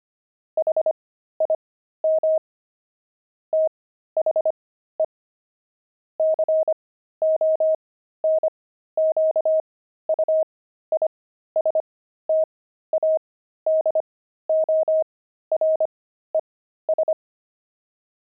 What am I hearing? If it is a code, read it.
Morse code, heard as a keyed tone, about 25 wpm, spaced as for 13 wpm, 630 Hz. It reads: HIM THE CONQUISTADORES
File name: pangram-fs-13-cs-25-kidding